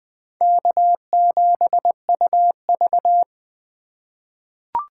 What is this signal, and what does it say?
Morse code, heard as a keyed tone, about 20 wpm, 700 Hz. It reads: K7UV
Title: USA Call Signs - 4 Characters - 3x Character Spacing - 20wpm